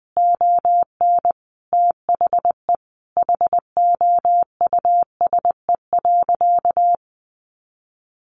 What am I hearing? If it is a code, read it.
Morse code, heard as a keyed tone, about 20 wpm, 700 Hz. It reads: ON THE HOUSE.
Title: Sentences from Top 1400 Words 20wpm